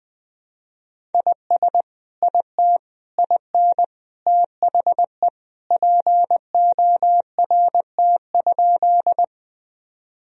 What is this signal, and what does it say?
Morse code, heard as a keyed tone, about 21 wpm, 700 Hz. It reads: IS IT IN THE PORT?
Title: Sentences from Top 1000 Words - Review Part 1 - 20wpm